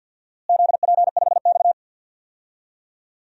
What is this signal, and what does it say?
Morse code, heard as a keyed tone, about 49 wpm, 700 Hz. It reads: ZP5X
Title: DXCC-100 Call Signs - 4 Characters - Repeated 2x - 50wpm